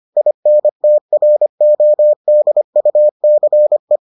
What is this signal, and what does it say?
Morse code, heard as a keyed tone, about 25 wpm, 600 Hz. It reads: INTRODUCE